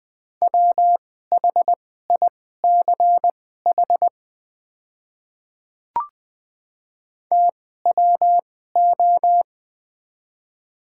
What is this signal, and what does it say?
Morse code, heard as a keyed tone, about 20 wpm, spaced as for 15 wpm, 700 Hz. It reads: WHICH TWO
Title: 2x Character Spacing - Top 100 Words - 20wpm